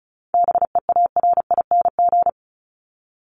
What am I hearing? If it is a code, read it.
Morse code, heard as a keyed tone, about 34 wpm, 700 Hz. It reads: BEARING
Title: ING Suffix - Top 100 Words with Base Words 35wpm